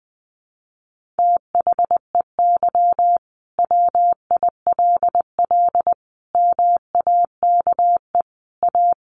Morse code, heard as a keyed tone, about 20 wpm, 700 Hz. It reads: THEY WILL MAKE A
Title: Sentences from Top 1500 Words 20wpm